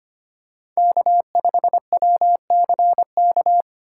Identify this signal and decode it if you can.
Morse code, heard as a keyed tone, about 25 wpm, 700 Hz. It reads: K5WCK
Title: USA Call Signs - 5 Characters - 3x Character Spacing - 25wpm